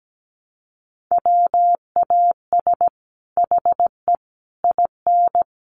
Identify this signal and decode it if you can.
Morse code, heard as a keyed tone, about 17 wpm, 700 Hz. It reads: WAS HE IN
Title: Sentences from Top 1300 Words 17wpm